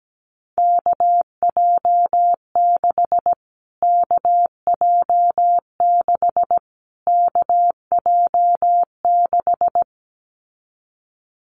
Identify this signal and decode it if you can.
Morse code, heard as a keyed tone, about 17 wpm, 700 Hz. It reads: KJ6 KJ6 KJ6
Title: Callsign Prefixes for Top 25 DXCC - Repeated 3x - 17wpm